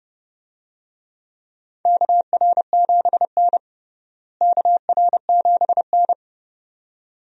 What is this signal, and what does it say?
Morse code, heard as a keyed tone, about 30 wpm, 700 Hz. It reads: KR7N KR7N
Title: USA Call Signs - 4 Characters - Repeated 2x - 30wpm